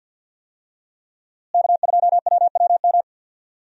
Morse code, heard as a keyed tone, about 51 wpm, 700 Hz. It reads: K2WWM
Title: USA Call Signs - 5 Characters - Repeated 2x - 50wpm